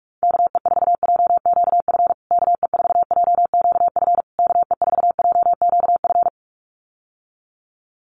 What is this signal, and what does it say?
Morse code, heard as a keyed tone, about 44 wpm, 700 Hz. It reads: KE4JQF KE4JQF KE4JQF
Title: USA Call Signs Repeated 3x 45wpm